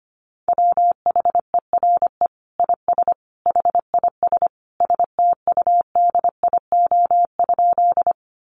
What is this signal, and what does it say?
Morse code, heard as a keyed tone, about 25 wpm, 700 Hz. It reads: WHERE IS HIS STUDIO?
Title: Sentences from Top 1200 Words 25wpm